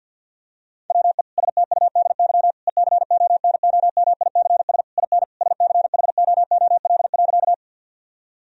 Morse code, heard as a keyed tone, about 50 wpm, 700 Hz. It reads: WE STUDY ECONOMICS IN SCHOOL.
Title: Sentences from Top 1000 Words - Review Part 3 - 50wpm